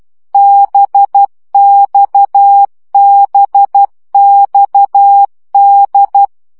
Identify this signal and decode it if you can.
Morse code, heard as a keyed tone, about 12 wpm, 800 Hz. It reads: BXBXD
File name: CW gyakorlas d b x 10 baud